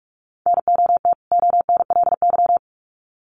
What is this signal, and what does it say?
Morse code, heard as a keyed tone, about 44 wpm, 700 Hz. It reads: NOT ONLY
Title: Sets of 2 Words 45wpm